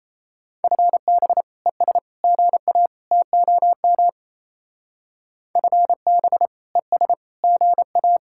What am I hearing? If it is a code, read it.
Morse code, heard as a keyed tone, about 34 wpm, 700 Hz. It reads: FB ES GA TOM FB ES GA
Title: Speed-Racing Common QSO Phrases - 28wpm